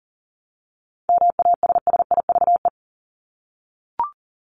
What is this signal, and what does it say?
Morse code, heard as a keyed tone, about 40 wpm, 700 Hz. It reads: MASSIVE
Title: Top 1751-2000 Words 40wpm